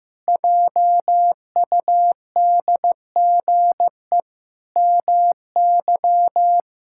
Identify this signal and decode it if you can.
Morse code, heard as a keyed tone, about 16 wpm, 700 Hz. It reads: JUDGE MY
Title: Pangram Challenge v2 - 15wpm